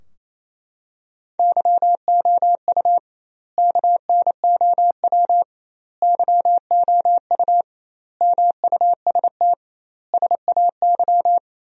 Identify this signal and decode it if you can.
Morse code, heard as a keyed tone, about 28 wpm, 700 Hz. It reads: YOU KNOW YOU MUST SAY
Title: 2x Word-Spacing - Sentences from Top 500 Words - 28wpm